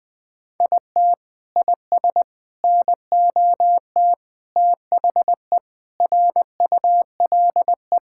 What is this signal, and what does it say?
Morse code, heard as a keyed tone, about 20 wpm, 700 Hz. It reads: IT IS NOT THE RULE
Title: Sentences from Top 1000 Words - Review Part 5 - 20wpm